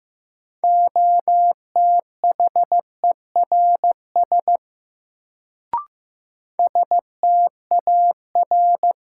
Morse code, heard as a keyed tone, about 15 wpm, 700 Hz. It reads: OTHERS STAR
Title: Sentences from Top 1000 Words - Review Part 2 - New Word Drill - 15wpm